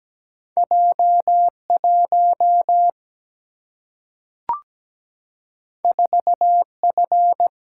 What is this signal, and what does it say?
Morse code, heard as a keyed tone, about 17 wpm, 700 Hz. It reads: J1 4F
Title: Single-Letter Single-Number 17wpm